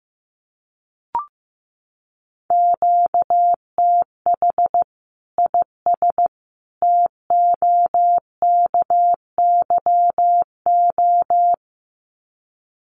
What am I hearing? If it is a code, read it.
Morse code, heard as a keyed tone, about 15 wpm, 700 Hz. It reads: QTH IS TOKYO
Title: Common QSO Phrases 15wpm